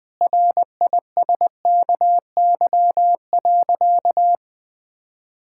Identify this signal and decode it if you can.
Morse code, heard as a keyed tone, about 21 wpm, 700 Hz. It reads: RISKY.
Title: Sentences with 6-Word Phrases - 20wpm